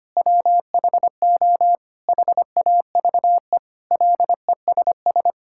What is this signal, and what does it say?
Morse code, heard as a keyed tone, about 25 wpm, 700 Hz. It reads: WHO HAVE LESS
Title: Sentences from Top 1600 Words 25wpm